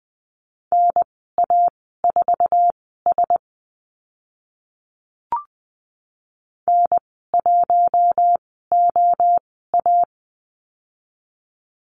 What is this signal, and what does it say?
Morse code, heard as a keyed tone, about 20 wpm, spaced as for 15 wpm, 700 Hz. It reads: NA4S N1OA
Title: 048 Lesson 9g - Number 4 - Call Signs 20wpm